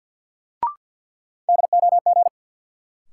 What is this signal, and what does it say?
Morse code, heard as a keyed tone, about 47 wpm, 700 Hz. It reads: DOG
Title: Sentences from Top 400 Words - New Word Drill - 50wpm